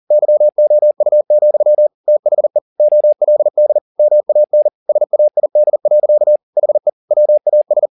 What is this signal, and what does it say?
Morse code, heard as a keyed tone, about 40 wpm, 600 Hz. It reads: YOU, THE OLD MAN SAID. HE WAS